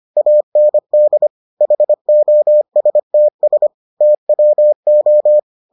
Morse code, heard as a keyed tone, about 25 wpm, 600 Hz. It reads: AND HOSTS TWO